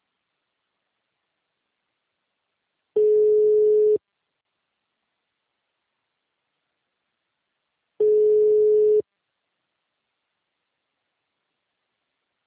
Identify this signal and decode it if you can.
Ringback tone (Europe, 425 Hz).
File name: out- -2011- - - .56733